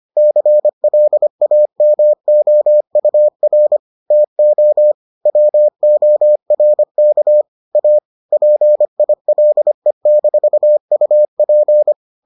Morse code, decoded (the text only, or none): CLAMOUR TO WORK A PILE-UP